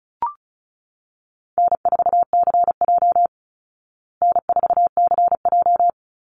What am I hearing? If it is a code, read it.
Morse code, heard as a keyed tone, about 34 wpm, 700 Hz. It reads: N4CJ N4CJ
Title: Easy Call Signs Repeated 3x - 35wpm